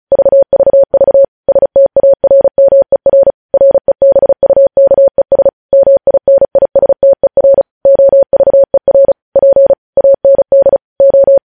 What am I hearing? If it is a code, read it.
Morse code, heard as a keyed tone, about 36 wpm, 550 Hz. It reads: VVV STARMER REBUKES MINISTER OVER P AND O